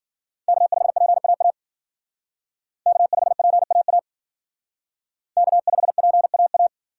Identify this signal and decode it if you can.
Morse code, heard as a keyed tone, about 50 wpm, 700 Hz. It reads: K5PAA K5PAA K5PAA
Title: Speed Racing - USA 1x3 Call Signs 40wpm